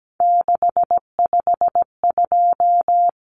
Morse code, heard as a keyed tone, about 17 wpm, 700 Hz. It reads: 652
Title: Numbers - 3-Digit 17wpm